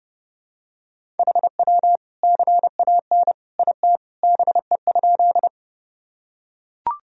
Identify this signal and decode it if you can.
Morse code, heard as a keyed tone, about 30 wpm, 700 Hz. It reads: HW CAN IT BE?
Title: Sentences from Top 100 Words and Abbreviations 30wpm